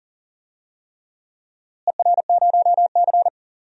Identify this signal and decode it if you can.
Morse code, heard as a keyed tone, about 40 wpm, 700 Hz. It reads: ER0C
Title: DXCC-100 Call Signs - 4 Characters - Repeated 3x - 40wpm